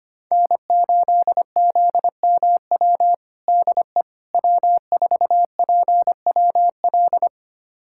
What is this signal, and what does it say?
Morse code, heard as a keyed tone, about 25 wpm, 700 Hz. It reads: N8ZMW DE W4PWL
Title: USA Callsign DE Callsign - 5 Characters - 25wpm